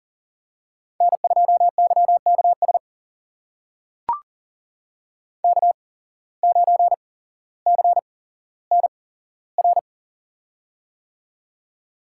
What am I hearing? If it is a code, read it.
Morse code, heard as a keyed tone, about 38 wpm, spaced as for 17 wpm, 700 Hz. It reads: N2YKS K9CNR
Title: USA Call Signs - 5 Characters - 8x Character Spacing - 40wpm